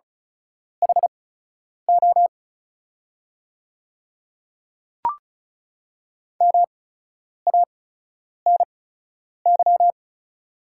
Morse code, heard as a keyed tone, about 36 wpm, spaced as for 10 wpm, 690 Hz. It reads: HO MANY
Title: 8x Character Spacing - Top 200 Words - 35wpm